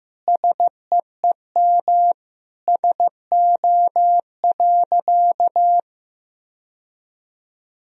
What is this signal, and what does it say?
Morse code, heard as a keyed tone, about 15 wpm, 700 Hz. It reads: SEEM SO.
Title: Sentences from Top 1000 Words - Review Part 1 - 15wpm